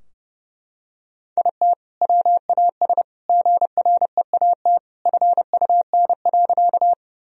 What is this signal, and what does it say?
Morse code, heard as a keyed tone, about 30 wpm, 700 Hz. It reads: IT WAS GREAT FUN.
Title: Sentences from Top 1000 Words - Review Part 4 - 30wpm